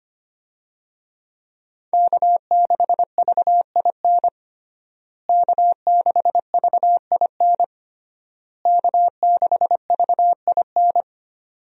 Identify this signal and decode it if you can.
Morse code, heard as a keyed tone, about 25 wpm, 700 Hz. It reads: K6VIN K6VIN K6VIN
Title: USA Call Signs - 5 Characters - Repeated 3x - 25wpm